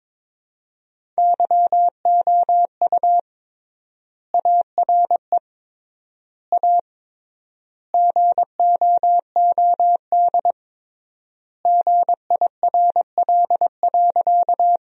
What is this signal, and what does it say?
Morse code, heard as a keyed tone, about 22 wpm, 700 Hz. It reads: YOU ARE A GOOD GIRL.
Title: 3x Word-Spacing - Sentences from Top 400 Words - 22wpm